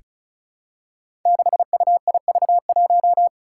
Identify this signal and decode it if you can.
Morse code, heard as a keyed tone, about 36 wpm, 700 Hz. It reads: 6UIV1
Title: Code Groups - 5 Letter:Numbers - 35wpm